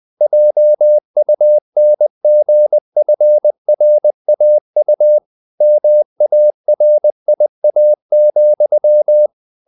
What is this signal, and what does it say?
Morse code, heard as a keyed tone, about 20 wpm, 600 Hz. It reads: JUNGFRAU MARIA,